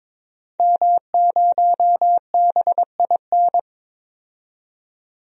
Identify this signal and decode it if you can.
Morse code, heard as a keyed tone, about 22 wpm, 700 Hz. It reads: M0BIN
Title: DXCC-100 Call Signs - 5 Characters - 22wpm